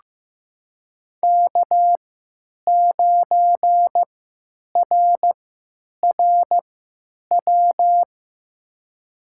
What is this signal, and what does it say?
Morse code, heard as a keyed tone, about 15 wpm, spaced as for 9 wpm, 700 Hz. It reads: K9RRW
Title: USA Call Signs - 5 Characters - 3x Character Spacing - 15wpm